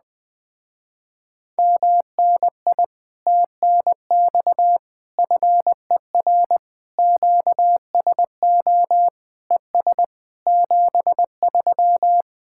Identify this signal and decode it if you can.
Morse code, heard as a keyed tone, about 20 wpm, 700 Hz. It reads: MNI TNX FER QSO ES 73
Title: Speed-Racing Common QSO Phrases - 17wpm